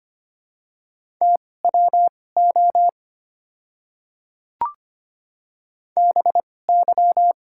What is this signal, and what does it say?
Morse code, heard as a keyed tone, about 25 wpm, spaced as for 18 wpm, 690 Hz. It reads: TWO BY
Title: 2x Character Spacing - Top 100 Words - 25wpm